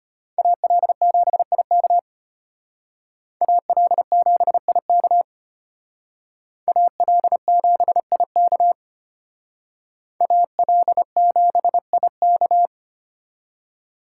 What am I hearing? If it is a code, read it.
Morse code, heard as a keyed tone, about 32 wpm, 700 Hz. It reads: AL7IK AL7IK AL7IK AL7IK